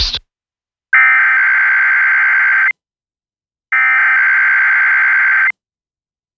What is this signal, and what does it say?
Emergency Alert System (SAME) tones (header bursts), heard as harsh buzzing bursts. It is heard from 0.9 to 5.8 s.